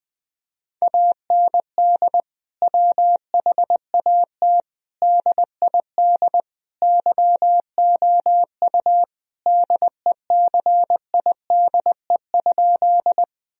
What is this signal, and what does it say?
Morse code, heard as a keyed tone, about 20 wpm, 700 Hz. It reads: AND WHAT DID YOU DECIDE?
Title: Sentences from Top 1000 Words - Review Part 3 - 20wpm